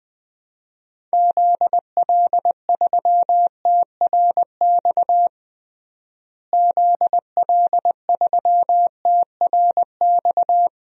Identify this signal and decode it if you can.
Morse code, heard as a keyed tone, about 20 wpm, 700 Hz. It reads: ZL3TRX ZL3TRX
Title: DXCC-100 Call Signs - 6 Characters - Repeated 3x - 20wpm